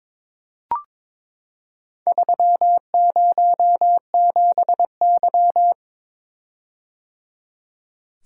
Morse code, heard as a keyed tone, about 22 wpm, 700 Hz. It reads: 307Y
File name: practice / 22wpm / Four Letter-Number Combinations - 22wpm